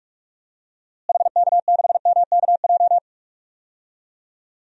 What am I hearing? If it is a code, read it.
Morse code, heard as a keyed tone, about 42 wpm, 700 Hz. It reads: HK6MKJ
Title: DXCC-100 Call Signs - 6 Characters - Repeated 3x - 45wpm